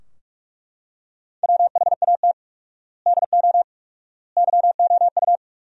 Morse code, heard as a keyed tone, about 41 wpm, 700 Hz. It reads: WHAT DO YOU
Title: 4x Word-Spacing - Sentences from Top 200 Words - 45wpm